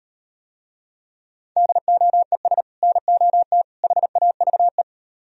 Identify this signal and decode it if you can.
Morse code, heard as a keyed tone, about 38 wpm, 700 Hz. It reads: DOES NOT HAVE